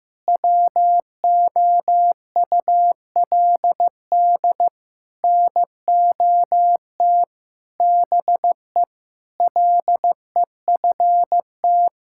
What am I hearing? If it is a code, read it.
Morse code, heard as a keyed tone, about 15 wpm, 700 Hz. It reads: WOULD NOT BE LEFT
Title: Sentences from Top 1000 Words - Review Part 5 - 15wpm